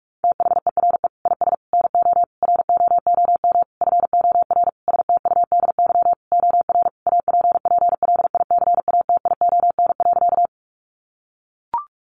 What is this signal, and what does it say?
Morse code, heard as a keyed tone, about 44 wpm, 700 Hz. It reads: THERE IS NO ROOM FOR STUDY OR APPLICATION.
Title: Sentences from Top 1200 Words 45wpm